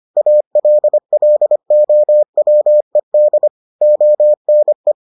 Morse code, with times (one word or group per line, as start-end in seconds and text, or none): 0.2-3.5 ALLOWED
3.8-4.9 ONE